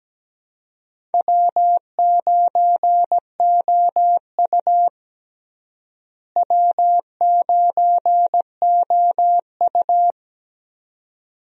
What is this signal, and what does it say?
Morse code, heard as a keyed tone, about 17 wpm, 700 Hz. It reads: W9OU W9OU
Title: USA Call Signs - 4 Characters - Repeated 2x - 17wpm